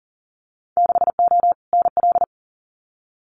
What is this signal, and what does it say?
Morse code, heard as a keyed tone, about 40 wpm, 700 Hz. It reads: 6O NL
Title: Winter Field Day Exchanges Encore 40wpm